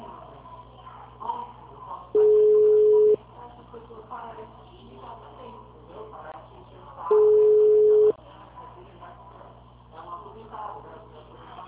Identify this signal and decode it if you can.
Ringback tone (Europe, 425 Hz), with speech.
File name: out- -2009- - - .7225